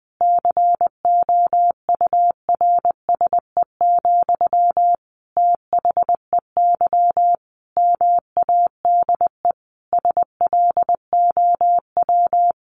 Morse code, heard as a keyed tone, about 20 wpm, 700 Hz. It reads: COURSE, THEY MADE SLOW
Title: Sentences from Top 1400 Words 20wpm